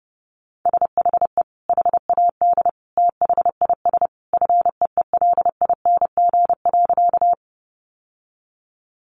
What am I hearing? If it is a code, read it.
Morse code, heard as a keyed tone, about 30 wpm, 700 Hz. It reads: SHE HAD THIS FEELING.
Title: Sentences from Top 1200 Words 30wpm